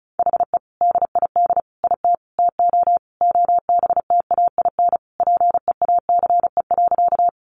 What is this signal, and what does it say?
Morse code, heard as a keyed tone, about 34 wpm, 700 Hz. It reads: HE DID IT TO OBTAIN PEACE.
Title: Sentences from Top 1500 Words 35wpm